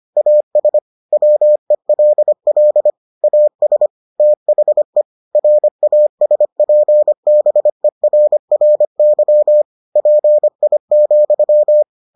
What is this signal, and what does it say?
Morse code, heard as a keyed tone, about 25 wpm, 600 Hz. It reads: AS WELL AS THE RASPBERRY PI,